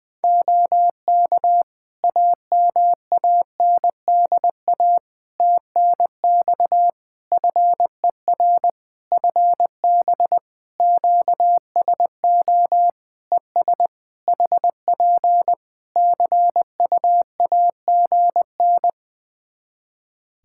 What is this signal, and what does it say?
Morse code, heard as a keyed tone, about 20 wpm, 700 Hz. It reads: OK AMANDA TNX FER FB QSO ES HP CUAGN
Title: Practice QSO - Exchange 4 - Ending - 20wpm